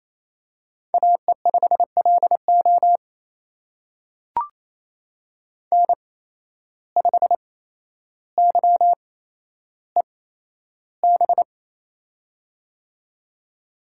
Morse code, heard as a keyed tone, about 27 wpm, spaced as for 12 wpm, 700 Hz. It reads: AE5LO N5YEB
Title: USA Call Signs - 5 Characters - 8x Character Spacing - 28wpm